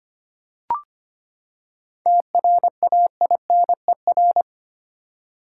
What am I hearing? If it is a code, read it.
Morse code, heard as a keyed tone, about 25 wpm, 700 Hz. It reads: TRAINER
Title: Seven Letter Words - 25wpm